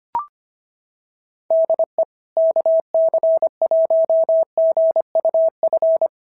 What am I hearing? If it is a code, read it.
Morse code, heard as a keyed tone, about 23 wpm, 650 Hz. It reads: DE KC1GUF
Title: SOTA QSO Practice 25wpm